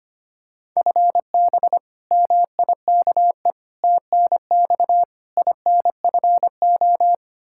Morse code, heard as a keyed tone, about 25 wpm, 700 Hz. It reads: FB MIKE TNX INFO
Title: Practice QSO - Exchange 3 - More Info - 25wpm